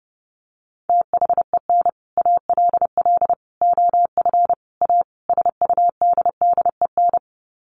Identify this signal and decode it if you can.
Morse code, heard as a keyed tone, about 30 wpm, 700 Hz. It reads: THEN ALL OF A SUDDEN
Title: Sets of 5 Words Encore 30wpm